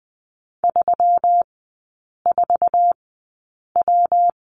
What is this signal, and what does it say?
Morse code, heard as a keyed tone, about 20 wpm, spaced as for 8 wpm, 690 Hz. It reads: 34W
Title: 172 Lesson 28e - Number 9 - Three Characters 20wpm